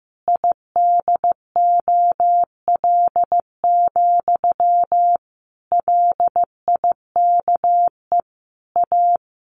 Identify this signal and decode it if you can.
Morse code, heard as a keyed tone, about 15 wpm, 700 Hz. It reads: IDOL, LIKE A
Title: A Sound Of Thunder 15wpm